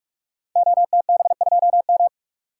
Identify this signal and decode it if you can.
Morse code, heard as a keyed tone, about 44 wpm, 690 Hz. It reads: OTB2M
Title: Code Groups - 5 Letter:Numbers - 45wpm